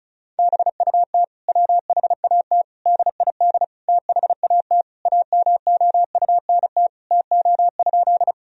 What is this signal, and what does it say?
Morse code, heard as a keyed tone, about 36 wpm, 700 Hz. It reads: BUT WHAT DID THAT AMOUNT TO?
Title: Sentences from Top 1000 Words - Review Part 4 - 35wpm